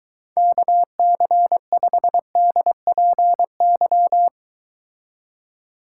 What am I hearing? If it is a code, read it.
Morse code, heard as a keyed tone, about 23 wpm, 700 Hz. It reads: KC5DPY